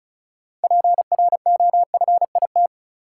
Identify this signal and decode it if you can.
Morse code, heard as a keyed tone, about 36 wpm, 700 Hz. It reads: PROFIT
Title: Sentences from Top 800 Words - New Word Drill - 35wpm